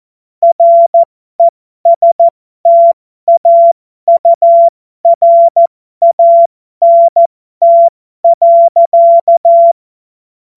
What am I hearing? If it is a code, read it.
Morse code, heard as a keyed tone, about 14 wpm, spaced as for 12 wpm, 680 Hz. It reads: RESTAURANT.
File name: Morsekurs / Witze